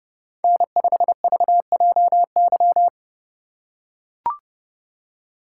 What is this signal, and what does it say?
Morse code, heard as a keyed tone, about 30 wpm, 700 Hz. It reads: N5VJY
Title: USA Call Signs - 5 Characters - 3x Character Spacing - 30wpm